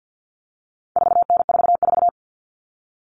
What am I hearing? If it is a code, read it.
Morse code, heard as a keyed tone, about 50 wpm, 700 Hz. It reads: 4N44
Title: All Cut Numbers - 4-Digit 50wpm